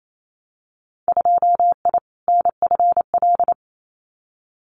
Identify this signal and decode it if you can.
Morse code, heard as a keyed tone, about 28 wpm, 700 Hz. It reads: 2I NFL
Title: Winter Field Day Exchanges 28wpm